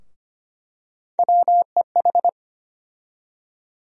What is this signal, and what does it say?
Morse code, heard as a keyed tone, about 25 wpm, 700 Hz. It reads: WEH
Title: USA Call Sign Postfixes - 3 Letters 25wpm